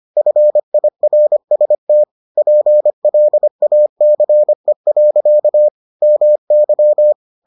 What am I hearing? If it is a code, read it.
Morse code, heard as a keyed tone, about 25 wpm, 600 Hz. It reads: FIRST PLACE. MY